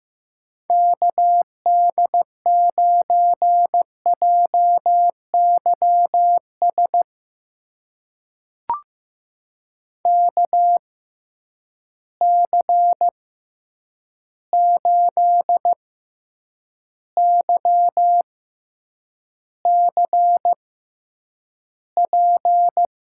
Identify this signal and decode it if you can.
Morse code, heard as a keyed tone, about 15 wpm, spaced as for 8 wpm, 700 Hz. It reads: KD9JYS KC8YCP
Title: USA Call Signs - 6 Characters - 6x Character Spacing - 15wpm